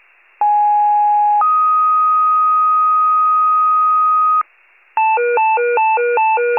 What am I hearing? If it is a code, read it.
Two-tone page, heard as a one-second tone, then a three-second tone, about 830 then 1250 Hz. It is heard from 0.4 to 4.4 s.
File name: pager / 74 FIRE / wcalert